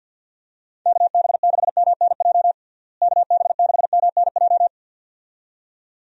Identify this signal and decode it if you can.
Morse code, heard as a keyed tone, about 52 wpm, 700 Hz. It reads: KB6MNJ KB6MNJ
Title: USA Call Signs - 6 Characters - Repeated 2x - 50wpm